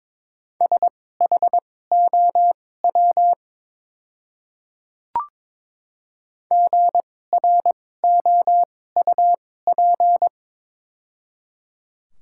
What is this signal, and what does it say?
Morse code, heard as a keyed tone, about 23 wpm, spaced as for 16 wpm, 700 Hz. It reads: SHOW GROUP
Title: 2x Character Spacing - Top 300 Words - 22wpm